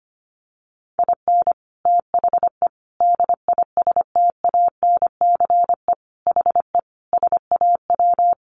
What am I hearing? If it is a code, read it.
Morse code, heard as a keyed tone, about 25 wpm, 700 Hz. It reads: IN THE DISTANCE HE SAW
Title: Sentences from Top 1200 Words 25wpm